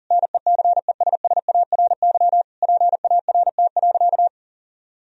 Morse code, heard as a keyed tone, about 40 wpm, 700 Hz. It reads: NECESSARY PART.
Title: Sentences from Top 1000 Words - Review Part 1 - 40wpm